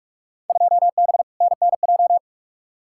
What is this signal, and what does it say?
Morse code, heard as a keyed tone, about 46 wpm, 700 Hz. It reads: 2B NNJ
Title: Field Day Exchanges Encore - 45wpm